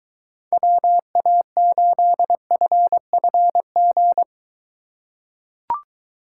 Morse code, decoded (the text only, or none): WA8FFG